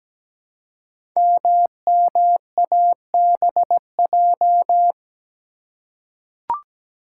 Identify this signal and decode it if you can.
Morse code, heard as a keyed tone, about 17 wpm, 700 Hz. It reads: MMABJ
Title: Code Groups - 5 Letter:Numbers - 17wpm